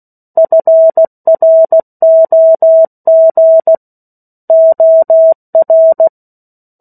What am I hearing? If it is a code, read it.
Morse code, heard as a keyed tone, about 16 wpm, 650 Hz. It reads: FROG OR
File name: 03-Chapter-03 16 WPM